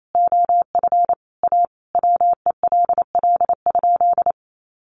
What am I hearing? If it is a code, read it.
Morse code, heard as a keyed tone, about 28 wpm, 700 Hz. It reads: OF A WELL?
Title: Sentences from Top 900 Words 28wpm